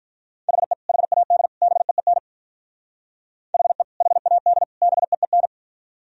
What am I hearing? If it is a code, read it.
Morse code, heard as a keyed tone, about 44 wpm, 700 Hz. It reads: HE HAD BEEN HE HAD BEEN